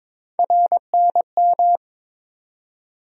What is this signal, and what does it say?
Morse code, heard as a keyed tone, about 22 wpm, 700 Hz. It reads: RNM
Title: USA Call Sign Postfixes - 3 Letters 22wpm